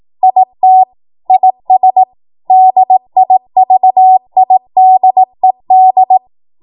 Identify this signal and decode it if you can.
Morse code, heard as a keyed tone, about 18 wpm, 750 Hz. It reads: IT IS DIVIDED
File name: Morse / 18WPM